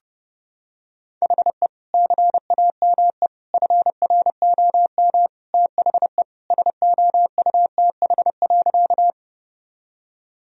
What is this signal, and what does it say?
Morse code, heard as a keyed tone, about 30 wpm, 700 Hz. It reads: HE CAME FROM THE SOUTH.
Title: Sentences from Top 1000 Words - Review Part 2 - 30wpm